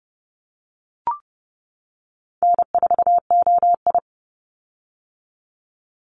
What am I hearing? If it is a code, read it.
Morse code, heard as a keyed tone, about 30 wpm, 700 Hz. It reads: N4OI
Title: Call Signs - 30wpm